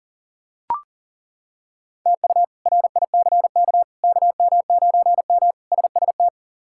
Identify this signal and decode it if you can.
Morse code, heard as a keyed tone, about 40 wpm, 700 Hz. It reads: TU RICK KM9M SST
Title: K1USN SST QSO Practice - 40wpm